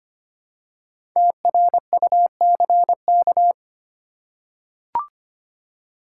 Morse code, heard as a keyed tone, about 25 wpm, 700 Hz. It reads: TRUCK
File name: 5 Letter Words - First 3 Letters Given - 25wpm